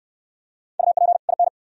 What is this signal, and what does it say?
Morse code, heard as a keyed tone, about 68 wpm, 700 Hz. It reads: UP IN